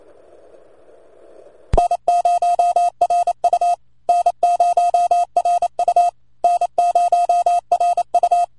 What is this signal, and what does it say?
Morse code, heard as a keyed tone, about 30 wpm, 700 Hz. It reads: N0RU N0RU N0RU